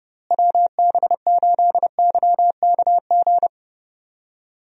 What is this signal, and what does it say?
Morse code, heard as a keyed tone, about 30 wpm, 700 Hz. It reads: WB8YKG
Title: USA Call Signs - 6 Characters - Repeated 2x - 30wpm